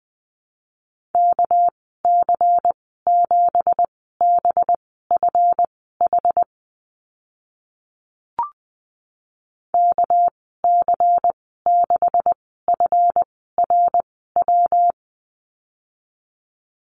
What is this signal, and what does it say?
Morse code, heard as a keyed tone, about 20 wpm, spaced as for 14 wpm, 700 Hz. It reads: KC7BFH KC6FRW
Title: 213 Lesson 34g - Number 8 - Call Signs 20wpm